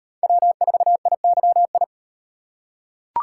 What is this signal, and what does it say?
Morse code, heard as a keyed tone, about 36 wpm, 700 Hz. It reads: W4IYI